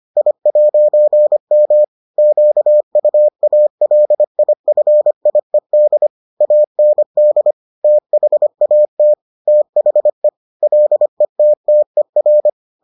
Morse code, heard as a keyed tone, about 25 wpm, 600 Hz. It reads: I'M QUALIFIED AND THAT THE LETTER